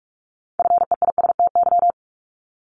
Morse code, heard as a keyed tone, about 46 wpm, 700 Hz. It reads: FEISTY
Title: Multi-letter Phonemes 45wpm